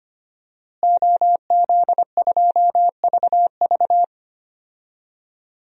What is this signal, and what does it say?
Morse code, heard as a keyed tone, about 25 wpm, 700 Hz. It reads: OZ2VV
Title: DXCC-100 Call Signs - 5 Characters - Repeated 2x - 25wpm